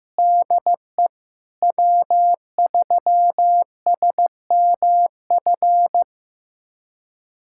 Morse code, heard as a keyed tone, about 15 wpm, 700 Hz. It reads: DE W3SMF
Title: USA Callsign DE Callsign - 5 Characters - 15wpm